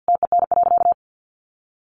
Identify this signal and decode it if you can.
Morse code, heard as a keyed tone, about 50 wpm, 700 Hz. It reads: TEN.
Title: Sentences from Top 1300 Words 50wpm